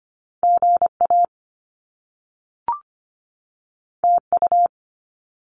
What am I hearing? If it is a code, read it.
Morse code, heard as a keyed tone, about 25 wpm, 700 Hz. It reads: GA TU
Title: Top Two Letter Combinations in Words 25wpm